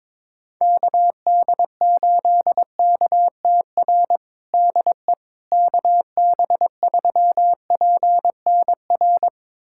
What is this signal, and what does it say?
Morse code, heard as a keyed tone, about 22 wpm, 700 Hz. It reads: KD8KTR DE KB3PNR
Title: USA Callsign DE Callsign - 6 Characters - 22wpm